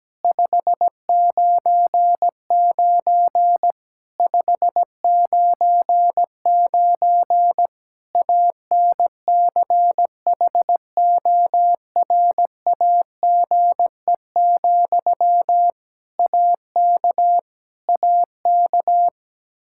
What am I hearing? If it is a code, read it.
Morse code, heard as a keyed tone, about 17 wpm, 700 Hz. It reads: 599 599 ANCHORAGE, AK AK
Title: Practice QSO - Exchange 1 - Beginning - 17wpm